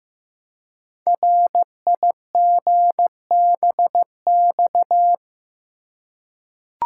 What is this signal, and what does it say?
Morse code, heard as a keyed tone, about 15 wpm, 700 Hz. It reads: RIGBX
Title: Code Groups - 5 Letters - 15wpm